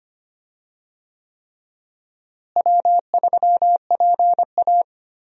Morse code, heard as a keyed tone, about 25 wpm, 700 Hz. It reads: W3PA
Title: USA Call Signs - 4 Characters - Repeated 3x - 25wpm